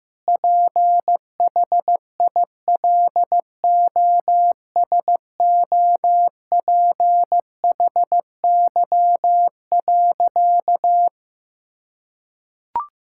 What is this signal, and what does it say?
Morse code, heard as a keyed tone, about 16 wpm, 700 Hz. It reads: PHILOSOPHY.
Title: Sentences from Top 1000 Words - Review Part 6 - 15wpm